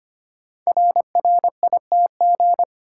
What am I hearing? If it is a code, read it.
Morse code, heard as a keyed tone, about 25 wpm, 700 Hz. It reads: RRITG
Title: Code Groups - 5 Letter:Numbers - 25wpm